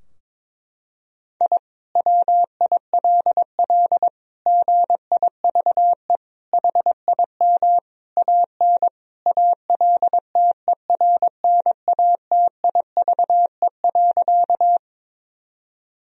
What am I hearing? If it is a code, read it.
Morse code, heard as a keyed tone, about 22 wpm, 700 Hz. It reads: I WILL GIVE HIM AN ALTERNATIVE.
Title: Sentences from Top 1000 Words - Review Part 4 - 22wpm